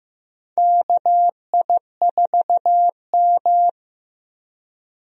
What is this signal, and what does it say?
Morse code, heard as a keyed tone, about 15 wpm, 700 Hz. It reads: KI4M
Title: USA Call Signs - 4 Characters - Repeated 2x - 15wpm